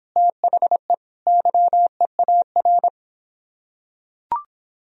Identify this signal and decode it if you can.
Morse code, heard as a keyed tone, about 25 wpm, 700 Hz. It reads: THE YEAR